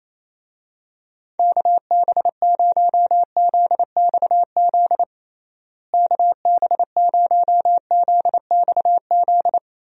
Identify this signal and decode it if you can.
Morse code, heard as a keyed tone, about 27 wpm, 700 Hz. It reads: KB0ZXZ KB0ZXZ
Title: USA Call Signs - 6 Characters - Repeated 3x - 28wpm